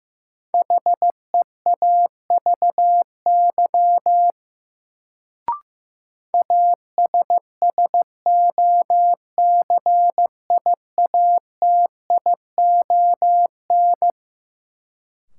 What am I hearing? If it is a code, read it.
Morse code, heard as a keyed tone, about 15 wpm, 700 Hz. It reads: HEAVY ASSOCIATION
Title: Sentences from Top 1100 Words - New Word Drill Alt - 15wpm